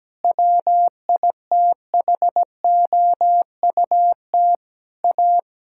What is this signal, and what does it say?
Morse code, heard as a keyed tone, about 17 wpm, 700 Hz. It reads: WITHOUT A
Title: Sentences from Top 1000 Words - Review Part 6 - 17wpm